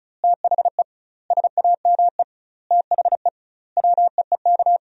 Morse code, heard as a keyed tone, about 36 wpm, 700 Hz. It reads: THE SAME THE WEEK
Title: 2x Word-Spacing - Sentences from Top 500 Words - 35wpm